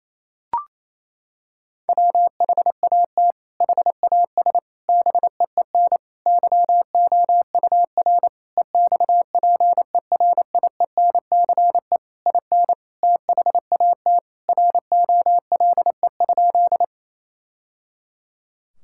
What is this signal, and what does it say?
Morse code, heard as a keyed tone, about 28 wpm, 700 Hz. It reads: WHAT HAS BEEN YOUR EXPERIENCE IN THAT ROLE?
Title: Sentences from Top 1000 Words - Review Part 6 - 28wpm